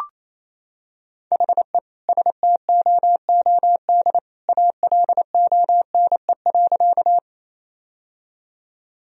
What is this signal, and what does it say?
Morse code, heard as a keyed tone, about 28 wpm, 700 Hz. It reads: HE STOOD ALONE.
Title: Sentences from Top 1000 Words - Review Part 2 - 28wpm